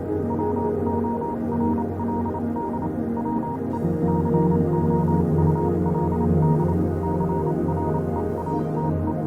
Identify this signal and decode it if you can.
Morse code, heard as a keyed tone, about 28 wpm, 940 Hz, with noise. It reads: WORLD WE COULD CREATE